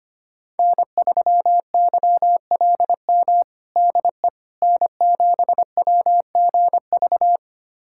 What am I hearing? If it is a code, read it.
Morse code, heard as a keyed tone, about 25 wpm, 700 Hz. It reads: N3YLM DE N7WGV
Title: USA Callsign DE Callsign - 5 Characters - 25wpm